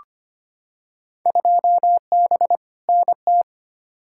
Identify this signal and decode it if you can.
Morse code, heard as a keyed tone, about 25 wpm, 700 Hz. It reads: 2B NT
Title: Field Day Exchanges Encore - 25wpm